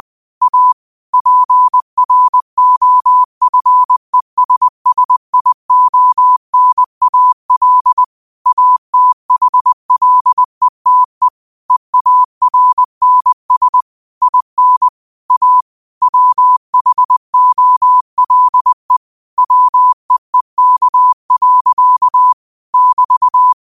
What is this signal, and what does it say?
Morse code, heard as a keyed tone, about 20 wpm, 1000 Hz. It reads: A PROFESSIONAL ATHLETE EARNS IN A WHOLE WEEK. <BT>
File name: qotdfiles / 20wpm